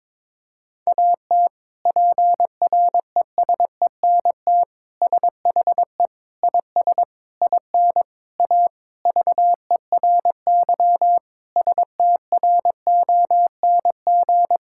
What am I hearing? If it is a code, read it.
Morse code, heard as a keyed tone, about 22 wpm, 700 Hz. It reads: AT PRESENT SHE IS IN A VERY STRONG
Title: Sentences from Top 1000 Words - Review Part 5 - 22wpm